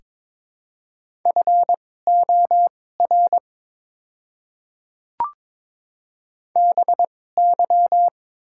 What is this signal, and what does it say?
Morse code, heard as a keyed tone, about 22 wpm, spaced as for 16 wpm, 700 Hz. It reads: FOR BY
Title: 2x Character Spacing - Top 100 Words - 22wpm